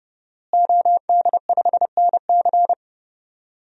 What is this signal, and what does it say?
Morse code, heard as a keyed tone, about 32 wpm, 700 Hz. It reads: OD5NC
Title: DXCC-100 Call Signs - 5 Characters - Repeated 2x - 30wpm